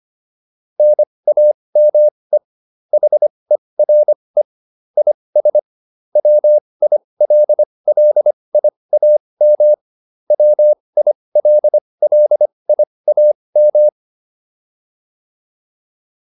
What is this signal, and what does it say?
Morse code, heard as a keyed tone, about 25 wpm, spaced as for 20 wpm, 600 Hz. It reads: NAME HERE IS WILLIAM WILLIAM